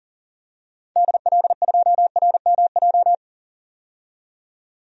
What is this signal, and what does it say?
Morse code, heard as a keyed tone, about 40 wpm, 700 Hz. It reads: DL2RMJ
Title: DXCC-100 Call Signs - 6 Characters - 40wpm